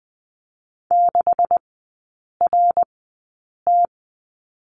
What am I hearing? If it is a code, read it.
Morse code, heard as a keyed tone, about 19 wpm, spaced as for 8 wpm, 700 Hz. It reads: 6RT
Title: 127 Lesson 21e - Number 6 - Three Characters 20wpm